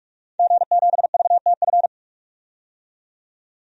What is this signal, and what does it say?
Morse code, heard as a keyed tone, about 43 wpm, 700 Hz. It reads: G7VTF
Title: DXCC-100 Call Signs - 5 Characters - 45wpm